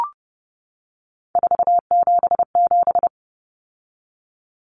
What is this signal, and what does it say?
Morse code, heard as a keyed tone, about 30 wpm, 700 Hz. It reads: 477
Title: Common Cut Numbers - 3-Digit 30wpm